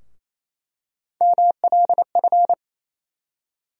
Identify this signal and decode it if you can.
Morse code, heard as a keyed tone, about 28 wpm, 700 Hz. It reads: MLF
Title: USA Call Sign Postfixes - 3 Letters 28wpm